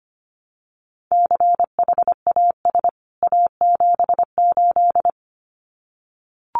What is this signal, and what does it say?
Morse code, heard as a keyed tone, about 25 wpm, 700 Hz. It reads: CHAS A78
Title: CWT Exchange 25wpm